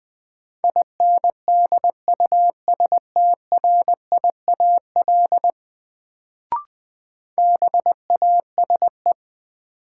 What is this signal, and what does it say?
Morse code, heard as a keyed tone, about 21 wpm, 700 Hz. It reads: INDUSTRIAL BASE
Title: Sets of 5 Words - New Word Drill - 20wpm